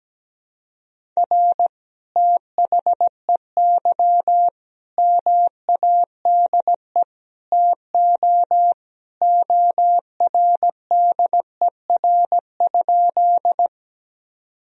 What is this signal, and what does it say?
Morse code, heard as a keyed tone, about 17 wpm, 700 Hz. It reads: R THEY MADE TO ORDER?
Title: Sentences from Top 400 Words and Abbreviations 17wpm